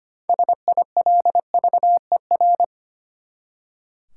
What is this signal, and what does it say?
Morse code, heard as a keyed tone, about 25 wpm, 700 Hz. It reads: SILVER
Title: Six Letter Words - 25wpm